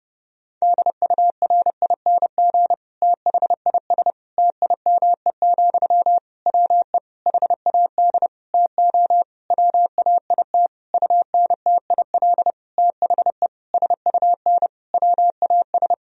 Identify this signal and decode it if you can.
Morse code, heard as a keyed tone, about 29 wpm, 700 Hz. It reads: DURING THIS TIME, WE HAD TO WAIT UNTIL THE SUN WAS
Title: Sentences from Top 1000 Words - Review Part 2 - 30wpm